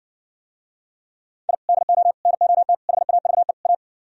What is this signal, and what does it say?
Morse code, heard as a keyed tone, about 59 wpm, 700 Hz. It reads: I DO NOT HAVE A